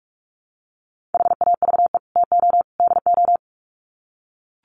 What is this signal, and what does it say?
Morse code, heard as a keyed tone, about 44 wpm, 700 Hz. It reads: HAVE TO DO
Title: Sets of 3 Words 45wpm